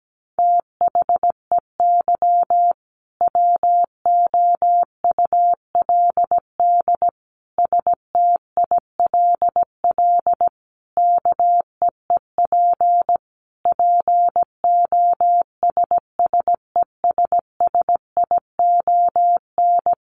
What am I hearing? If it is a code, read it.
Morse code, heard as a keyed tone, about 17 wpm, 700 Hz. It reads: THEY WOULD STILL KEEP POSSESSION
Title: Sentences from Top 1400 Words 17wpm